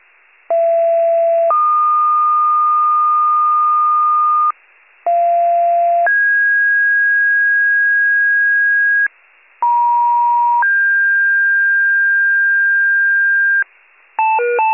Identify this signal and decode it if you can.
Two-tone page, heard as a one-second tone, then a three-second tone, about 670 then 1220 Hz, about 690 then 1650 Hz, about 950 then 1600 Hz, with noise. It is heard from 0.5 to 13.6 s.